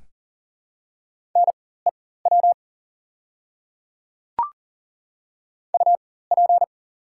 Morse code, heard as a keyed tone, about 40 wpm, spaced as for 19 wpm, 700 Hz. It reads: NEW UP
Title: 4x Character Spacing - Top 500 Words - 40wpm